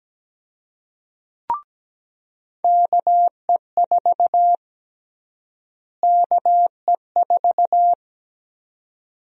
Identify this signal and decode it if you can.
Morse code, heard as a keyed tone, about 17 wpm, 700 Hz. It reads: KE4 KE4
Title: USA Call Sign Prefixes - Repeated 2x - 17wpm